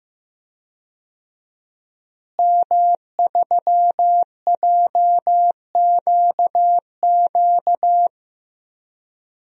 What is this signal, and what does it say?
Morse code, heard as a keyed tone, about 15 wpm, 700 Hz. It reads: M3JQQ
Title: DXCC-100 Call Signs - 5 Characters - Repeated 3x - 15wpm